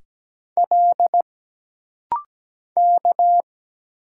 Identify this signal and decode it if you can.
Morse code, heard as a keyed tone, about 17 wpm, 700 Hz. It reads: L K
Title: Single Letters - Rapid-Fire 17wpm